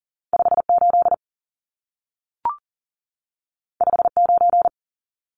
Morse code, heard as a keyed tone, about 40 wpm, 700 Hz. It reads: 58 59
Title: Numbers - 2-Digit 40wpm